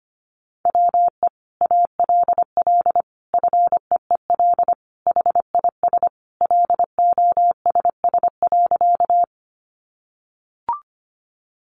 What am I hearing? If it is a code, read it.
Morse code, heard as a keyed tone, about 25 wpm, 700 Hz. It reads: WE ALL FEEL HIS LOSS.
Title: Sentences from Top 1400 Words 25wpm